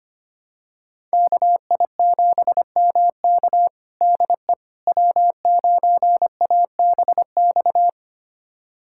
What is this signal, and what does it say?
Morse code, heard as a keyed tone, about 25 wpm, 700 Hz. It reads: KI7MK DE W9ABX
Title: USA Callsign DE Callsign - 5 Characters - 25wpm